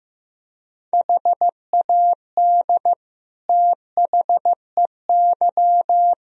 Morse code, heard as a keyed tone, about 15 wpm, 700 Hz. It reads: HAD THEY
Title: Sentences from Top 1000 Words - Review Part 2 - 15wpm